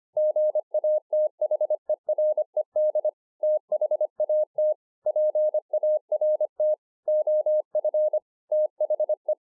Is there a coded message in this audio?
Yes: Morse code from 0.2 to 9.3 s, heard as a keyed tone, about 25 wpm, 610 Hz. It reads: GATHERED THAT PART OF THE